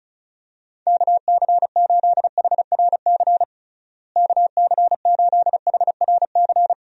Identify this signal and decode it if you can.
Morse code, heard as a keyed tone, about 36 wpm, 700 Hz. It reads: KC8HRC KC8HRC
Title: USA Call Signs - 6 Characters - Repeated 3x - 35wpm